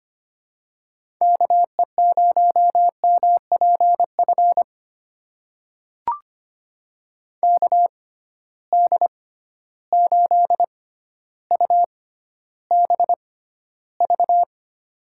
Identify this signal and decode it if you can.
Morse code, heard as a keyed tone, about 25 wpm, spaced as for 13 wpm, 700 Hz. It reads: KE0MPF KD8UBV
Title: USA Call Signs - 6 Characters - 6x Character Spacing - 25wpm